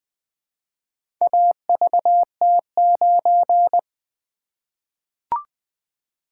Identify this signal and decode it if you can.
Morse code, heard as a keyed tone, about 20 wpm, 700 Hz. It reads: AVT9
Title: Four Letter-Number Combinations - 20wpm